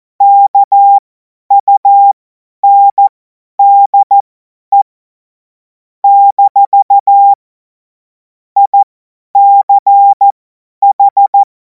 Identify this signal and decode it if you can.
Morse code, heard as a keyed tone, about 14 wpm, spaced as for 10 wpm, 800 Hz. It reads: KUNDE - ICH
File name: Morsekurs / Witze / Witz 17. Februar 2026 50 F70